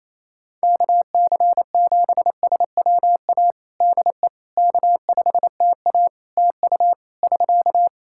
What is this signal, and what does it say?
Morse code, heard as a keyed tone, about 28 wpm, 700 Hz. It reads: KC7SWA DE K5TA TU <SK>
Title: Practice QSO - Exchange 4 - Ending - 28wpm